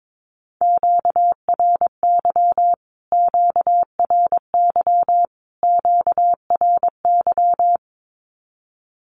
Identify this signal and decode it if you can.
Morse code, heard as a keyed tone, about 22 wpm, 700 Hz. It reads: QRY QRY QRY
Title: Q-Codes Repeated 3x 22wpm